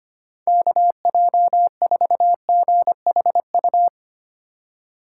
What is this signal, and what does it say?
Morse code, heard as a keyed tone, about 32 wpm, 700 Hz. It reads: KJ4GHU